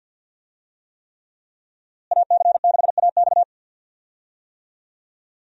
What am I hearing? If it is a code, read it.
Morse code, heard as a keyed tone, about 51 wpm, 700 Hz. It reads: AC6AX
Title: USA Call Signs - 5 Characters - Repeated 3x - 50wpm